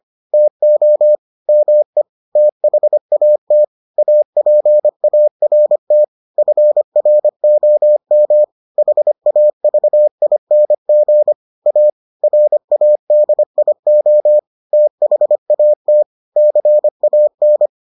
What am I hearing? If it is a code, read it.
Morse code, heard as a keyed tone, about 25 wpm, 600 Hz. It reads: TO ME THAT APART FROM HAVING A RADIO THAT CAN